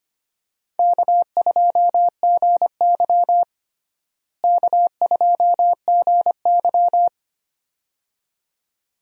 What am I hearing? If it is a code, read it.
Morse code, heard as a keyed tone, about 25 wpm, 700 Hz. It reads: K2GY K2GY
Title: USA Call Signs - 4 Characters - Repeated 2x - 25wpm